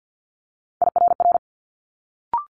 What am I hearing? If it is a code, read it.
Morse code, heard as a keyed tone, about 50 wpm, 700 Hz. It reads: IRR
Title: Multi-letter Phonemes Quick Fire 50wpm